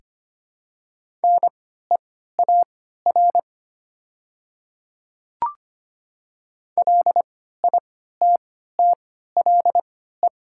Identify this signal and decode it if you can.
Morse code, heard as a keyed tone, about 25 wpm, spaced as for 14 wpm, 700 Hz. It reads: NEAR LITTLE
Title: 3x Character Spacing - Top 300 Words - 25wpm